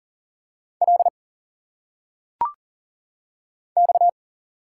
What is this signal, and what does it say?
Morse code, heard as a keyed tone, about 40 wpm, 700 Hz. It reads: L X
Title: Single Letters - Phonetic Pronunciation 40wpm